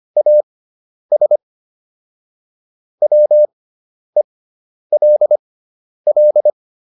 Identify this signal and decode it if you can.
Morse code, heard as a keyed tone, about 25 wpm, spaced as for 10 wpm, 600 Hz. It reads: AS WELL